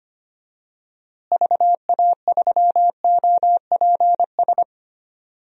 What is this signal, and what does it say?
Morse code, heard as a keyed tone, about 25 wpm, 700 Hz. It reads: VA3OPS
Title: DXCC-100 Call Signs - 6 Characters - Repeated 2x - 25wpm